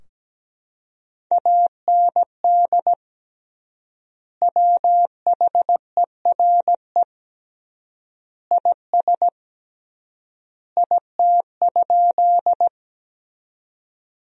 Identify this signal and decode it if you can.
Morse code, heard as a keyed tone, about 17 wpm, 700 Hz. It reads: AND WHERE IS IT?
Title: 3x Word-Spacing - Sentences from Top 200 Words - 17wpm